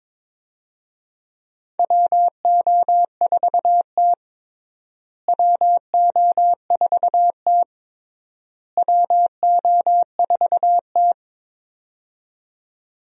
Morse code, heard as a keyed tone, about 22 wpm, 700 Hz. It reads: WO4T WO4T WO4T
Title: USA Call Signs - 4 Characters - Repeated 3x - 22wpm